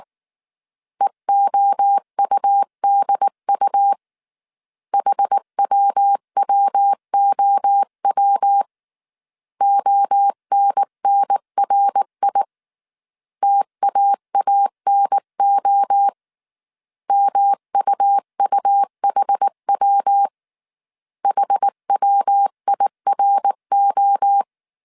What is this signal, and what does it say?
Morse code, heard as a keyed tone, about 19 wpm, 800 Hz. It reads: EOUDU HWWOW ONNRI TAANO MUUHW HWIRO
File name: CW / DET / 1 3 1500C S B 18wpm 1916